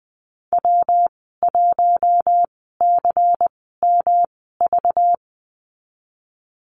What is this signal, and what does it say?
Morse code, heard as a keyed tone, about 20 wpm, spaced as for 14 wpm, 700 Hz. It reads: W1CMV
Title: 193 Lesson 31g - Letter v - Call Signs 20wpm